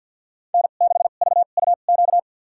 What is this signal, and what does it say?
Morse code, heard as a keyed tone, about 52 wpm, spaced as for 36 wpm, 700 Hz. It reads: N6VUQ
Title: USA Call Signs - 2x Character Spacing - 50wpm